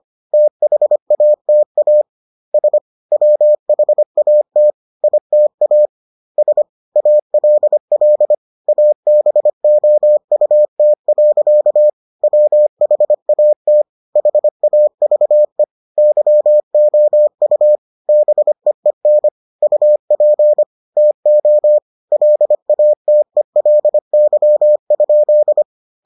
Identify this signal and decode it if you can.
Morse code, heard as a keyed tone, about 25 wpm, 600 Hz. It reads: THATAS WHAT ITAS ALL ABOUT. WHAT HAVE YOU BEEN UP TO LATELY?